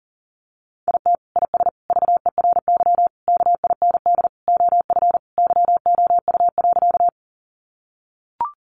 Morse code, heard as a keyed tone, about 40 wpm, 700 Hz. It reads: IT IS VERY KIND OF YOU.
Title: Sentences from Top 200 Words 40wpm